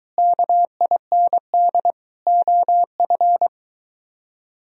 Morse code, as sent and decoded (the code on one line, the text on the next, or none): -.- .. -. -.. / --- ..-.
KIND OF